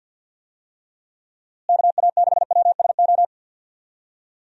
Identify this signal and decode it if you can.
Morse code, heard as a keyed tone, about 48 wpm, 700 Hz. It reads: KA6WSO
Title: USA Call Signs - 6 Characters - Repeated 2x - 50wpm